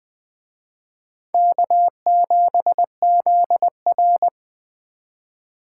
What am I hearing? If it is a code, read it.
Morse code, heard as a keyed tone, about 21 wpm, 700 Hz. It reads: K7ZR